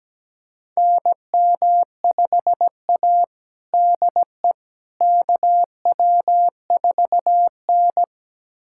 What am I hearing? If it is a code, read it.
Morse code, heard as a keyed tone, about 17 wpm, 700 Hz. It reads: NM5A DE KW4N
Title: USA Callsign DE Callsign - 4 Characters - 17wpm